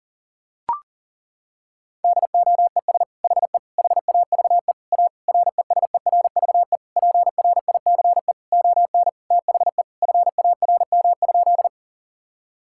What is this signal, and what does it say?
Morse code, heard as a keyed tone, about 40 wpm, 700 Hz. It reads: DOES HE HAVE A RESERVE PRICE ON THE FARM?
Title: Sentences from Top 1000 Words - Review Part 6 - 40wpm